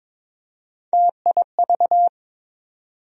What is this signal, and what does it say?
Morse code, heard as a keyed tone, about 22 wpm, 700 Hz. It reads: TIV
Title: USA Call Sign Postfixes - 3 Letters 22wpm